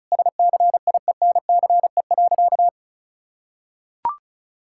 Morse code, heard as a keyed tone, about 36 wpm, 700 Hz. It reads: SCIENCE.
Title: Sentences from Top 1000 Words - Review Part 6 - 35wpm